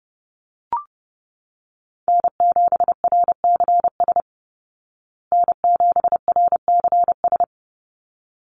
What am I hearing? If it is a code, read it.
Morse code, heard as a keyed tone, about 30 wpm, 700 Hz. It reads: N7RCS N7RCS
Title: Call Signs Repeated 3x - 30wpm